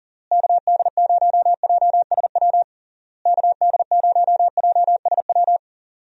Morse code, heard as a keyed tone, about 40 wpm, 700 Hz. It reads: KD0JSW KD0JSW
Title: USA Call Signs - 6 Characters - Repeated 3x - 40wpm